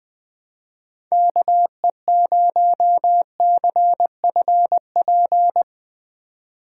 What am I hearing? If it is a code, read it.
Morse code, heard as a keyed tone, about 20 wpm, 700 Hz. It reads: KE0CFP
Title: USA Call Signs - 2x Character Spacing - 20wpm